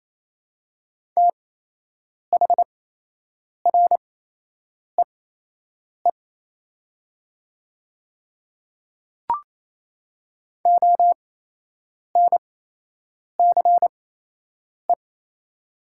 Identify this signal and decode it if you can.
Morse code, heard as a keyed tone, about 27 wpm, spaced as for 8 wpm, 700 Hz. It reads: THREE ONCE
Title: 8x Character Spacing - Top 300 Words - 28wpm